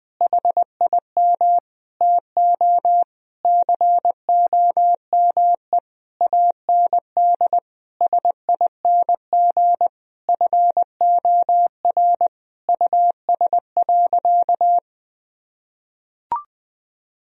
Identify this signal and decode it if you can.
Morse code, heard as a keyed tone, about 20 wpm, 700 Hz. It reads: HIM TO COME AND SING FOR US.
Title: Sentences from Top 1000 Words - Review Part 2 - 20wpm